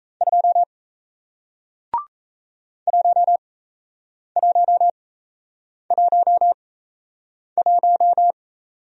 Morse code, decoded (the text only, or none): 2 1111